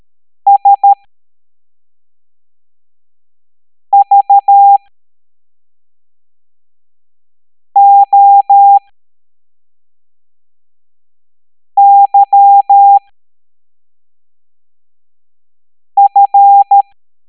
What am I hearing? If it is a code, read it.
Morse code, heard as a keyed tone, about 13 wpm, spaced as for 3 wpm, 800 Hz. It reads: SVOYF